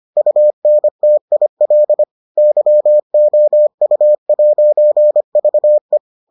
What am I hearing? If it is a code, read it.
Morse code, heard as a keyed tone, about 25 wpm, 600 Hz. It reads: UNTIL YOU'VE